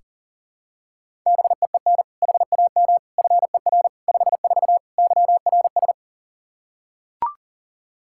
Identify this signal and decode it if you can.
Morse code, heard as a keyed tone, about 40 wpm, 700 Hz. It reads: BEEN HAM FER 54 YRS
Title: Practice QSO - Exchange 3 - More Info - 40wpm